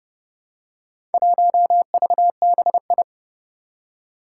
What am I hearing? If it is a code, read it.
Morse code, heard as a keyed tone, about 30 wpm, 700 Hz. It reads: 1VBI
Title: Four Challenging Letter-Number Combinations 30wpm